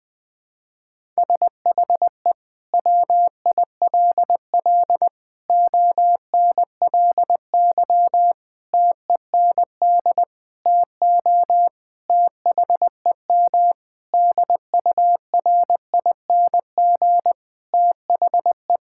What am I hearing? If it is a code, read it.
Morse code, heard as a keyed tone, about 20 wpm, 700 Hz. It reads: SHE WILL ONLY TEND TO THEM DURING THE
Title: Sentences from Top 1000 Words - Review Part 5 - 20wpm